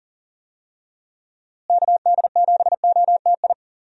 Morse code, heard as a keyed tone, about 37 wpm, 700 Hz. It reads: KD7OTI
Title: USA Call Signs - 6 Characters - Repeated 2x - 40wpm